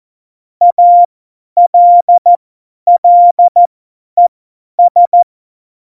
Morse code, heard as a keyed tone, about 14 wpm, spaced as for 10 wpm, 700 Hz. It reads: ALLES
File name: Morsekurs / Witze / Witz 24. November 2020, 50 F70